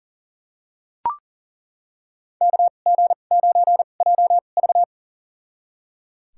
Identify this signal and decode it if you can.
Morse code, heard as a keyed tone, about 37 wpm, spaced as for 28 wpm, 700 Hz. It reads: KG9JV
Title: USA Call Signs - 2x Character Spacing - 40wpm